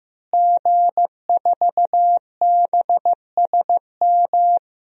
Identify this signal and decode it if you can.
Morse code, heard as a keyed tone, about 15 wpm, 700 Hz. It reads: G4BSM
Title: DXCC-100 Call Signs - 5 Characters - Repeated 3x - 15wpm